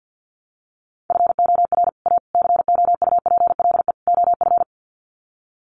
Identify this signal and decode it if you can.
Morse code, heard as a keyed tone, about 50 wpm, 700 Hz. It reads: FOR A COUPLE OF
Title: Sets of 4 Words 50wpm